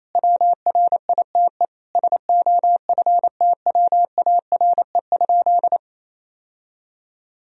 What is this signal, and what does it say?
Morse code, heard as a keyed tone, about 28 wpm, 700 Hz. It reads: WRITE SOFTWARE?
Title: Sentences from Top 1000 Words - Review Part 3 - 28wpm